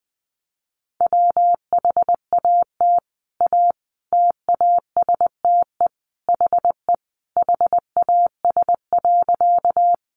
Morse code, heard as a keyed tone, about 20 wpm, 700 Hz. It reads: WHAT A TASTE HE HAS.
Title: Sentences from Top 1100 Words 20wpm